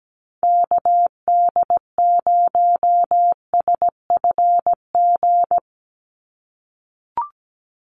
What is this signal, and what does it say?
Morse code, heard as a keyed tone, about 17 wpm, 700 Hz. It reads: KD0SFG
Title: USA Call Signs 17wpm